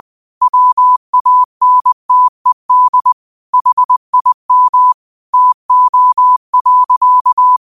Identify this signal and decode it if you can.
Morse code, heard as a keyed tone, about 20 wpm, 1000 Hz. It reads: WANTED HIM TO.
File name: qotdfiles / 20wpm